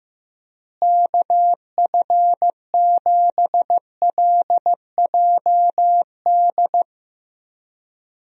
Morse code, heard as a keyed tone, about 15 wpm, 700 Hz. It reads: KF7LJD
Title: USA Call Signs - 6 Characters - Repeated 2x - 15wpm